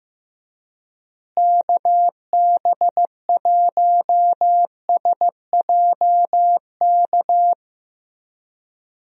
Morse code, heard as a keyed tone, about 16 wpm, 700 Hz. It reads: KB1SJK
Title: USA Call Signs - 6 Characters - Repeated 2x - 15wpm